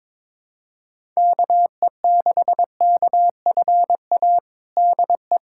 Morse code, heard as a keyed tone, about 22 wpm, 700 Hz. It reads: KE6KFA DE
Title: USA Callsign DE Callsign - 6 Characters - 22wpm